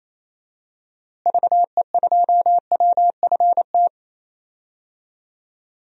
Morse code, heard as a keyed tone, about 28 wpm, 700 Hz. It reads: VE2WFT
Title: DXCC-100 Call Signs - 6 Characters - Repeated 3x - 28wpm